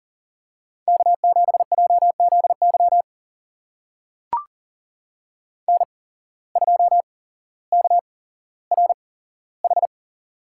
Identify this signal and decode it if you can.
Morse code, heard as a keyed tone, about 40 wpm, spaced as for 17 wpm, 700 Hz. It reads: K7JZY N2KRH
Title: USA Call Signs - 5 Characters - 8x Character Spacing - 40wpm